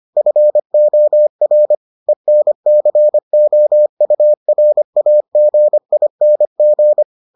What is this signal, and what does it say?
Morse code, heard as a keyed tone, about 25 wpm, 600 Hz. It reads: FOR ENCOURAGING